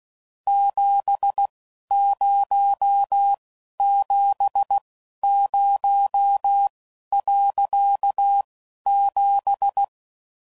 Morse code, heard as a keyed tone, about 16 wpm, spaced as for 11 wpm, 800 Hz. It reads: 7070.7
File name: CW Code Groups @10WPM / QSOs / QSO kc7sac